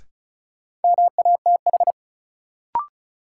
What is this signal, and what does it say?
Morse code, heard as a keyed tone, about 36 wpm, 700 Hz. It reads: MATH
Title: Sentences from Top 1000 Words - Review Part 6 - New Word Drill - 35wpm